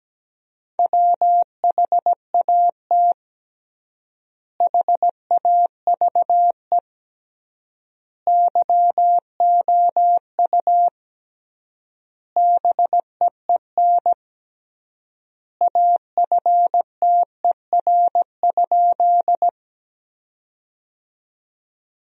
Morse code, heard as a keyed tone, about 17 wpm, 700 Hz. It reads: WHAT HAVE YOU BEEN AFTER?
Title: 3x Word-Spacing - Sentences from Top 200 Words - 17wpm